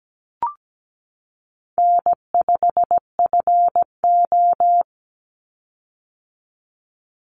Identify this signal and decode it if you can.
Morse code, heard as a keyed tone, about 17 wpm, 700 Hz. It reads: N5FO
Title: Easy Call Signs - 17wpm